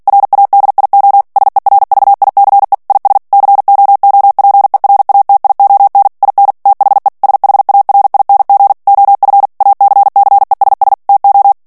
Morse code, heard as a keyed tone, about 48 wpm, 800 Hz. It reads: RADIO SERVICE IS COOPERATION IN THE SHARING OF ACCESS TO
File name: ANGLAIS texte clair 3 vitesse 45 mots minute